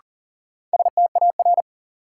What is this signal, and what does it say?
Morse code, heard as a keyed tone, about 37 wpm, 700 Hz. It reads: STAR
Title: Sentences from Top 1000 Words - Review Part 2 - New Word Drill - 40wpm